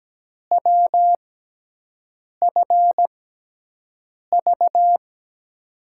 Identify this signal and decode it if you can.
Morse code, heard as a keyed tone, about 17 wpm, spaced as for 6 wpm, 700 Hz. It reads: WFV
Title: USA Call Signs - 5 Characters - 6x Character Spacing - 17wpm